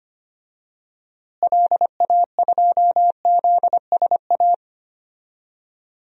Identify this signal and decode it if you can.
Morse code, heard as a keyed tone, about 25 wpm, 700 Hz. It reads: LA2ZSA
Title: DXCC-100 Call Signs - 6 Characters - Repeated 2x - 25wpm